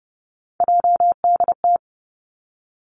Morse code, heard as a keyed tone, about 30 wpm, 700 Hz. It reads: JDT
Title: Three Challenging Letter-Number Combinations 30wpm